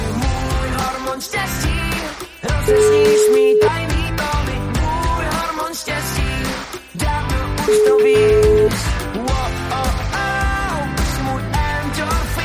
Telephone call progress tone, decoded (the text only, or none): ringback tone